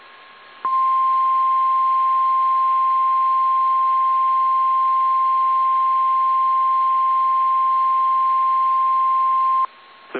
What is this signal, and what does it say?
Emergency Alert System (SAME) tones (the 1050 Hz attention tone). It is heard from 0.6 to 9.7 s.
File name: eas / KSDZKDJL / KSDZKDJL